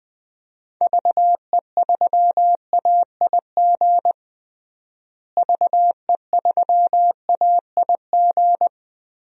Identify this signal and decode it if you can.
Morse code, heard as a keyed tone, about 21 wpm, 700 Hz. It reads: VE3AIG VE3AIG
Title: DXCC-100 Call Signs - 6 Characters - Repeated 3x - 20wpm